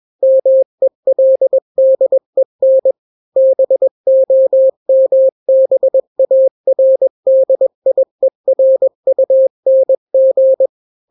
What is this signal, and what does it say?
Morse code, heard as a keyed tone, about 21 wpm, 530 Hz. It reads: MELDEN BOMBARDIERUNG
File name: daily / 20wpm